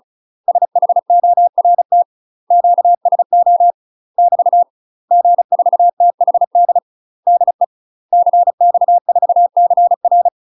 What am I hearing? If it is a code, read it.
Morse code, heard as a keyed tone, about 35 wpm, 700 Hz. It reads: SHORT QSO <BT> G4THD DE CX4CR
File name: UK MASTER QSOs TRAINING/35WPM